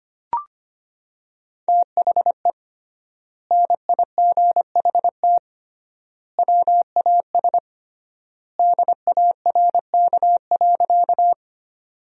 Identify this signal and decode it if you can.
Morse code, heard as a keyed tone, about 25 wpm, 700 Hz. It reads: THE NIGHT WAS DARK.
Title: 3x Word-Spacing - Sentences from Top 500 Words - 25wpm